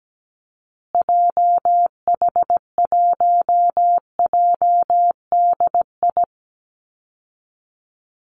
Morse code, heard as a keyed tone, about 17 wpm, 700 Hz. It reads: JH1JDI
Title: Call Signs Repeated 2x - 17wpm